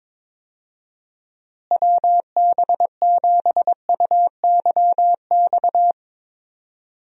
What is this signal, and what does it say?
Morse code, heard as a keyed tone, about 22 wpm, 700 Hz. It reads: WB7UYX
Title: USA Call Signs - 6 Characters - Repeated 3x - 22wpm